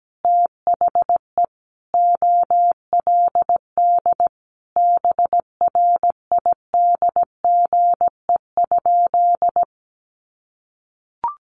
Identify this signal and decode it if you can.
Morse code, heard as a keyed tone, about 17 wpm, 700 Hz. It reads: THE OLD BRIDGE?
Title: Sentences from Top 1200 Words 17wpm